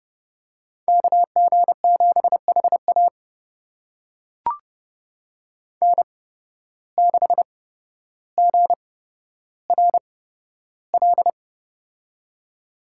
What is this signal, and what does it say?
Morse code, heard as a keyed tone, about 30 wpm, spaced as for 13 wpm, 700 Hz. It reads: KG7HA N6GRL
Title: USA Call Signs - 5 Characters - 8x Character Spacing - 30wpm